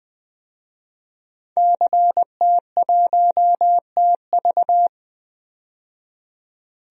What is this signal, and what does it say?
Morse code, heard as a keyed tone, about 20 wpm, 700 Hz. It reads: CT1TV
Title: DXCC-100 Call Signs - 5 Characters - 20wpm